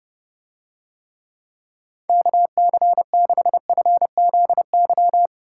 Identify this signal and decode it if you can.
Morse code, heard as a keyed tone, about 29 wpm, 700 Hz. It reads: KC6FZY